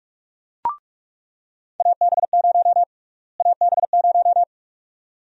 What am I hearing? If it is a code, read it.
Morse code, heard as a keyed tone, about 46 wpm, 700 Hz. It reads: AB0 AB0
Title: USA Call Sign Prefixes - Repeated 2x - 45wpm